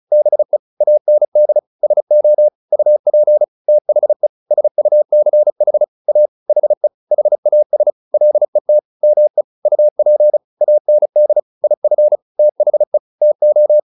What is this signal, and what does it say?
Morse code, heard as a keyed tone, about 34 wpm, 600 Hz. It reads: BE AND SO UP THE SUCH A HE HAS LET ME UP AND IF THE TO